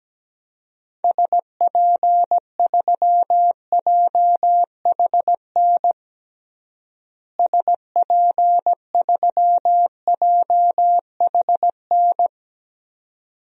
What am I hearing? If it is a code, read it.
Morse code, heard as a keyed tone, about 17 wpm, 700 Hz. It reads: SP3JHN SP3JHN
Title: DXCC-100 Call Signs - 6 Characters - Repeated 2x - 17wpm